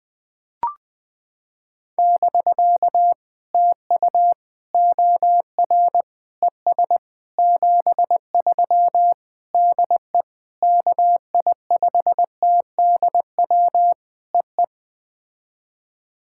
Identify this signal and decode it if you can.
Morse code, heard as a keyed tone, about 18 wpm, 700 Hz. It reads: <BK> TU OR ES 73 DE KI5TDW EE
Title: POTA QSO Practice - 20wpm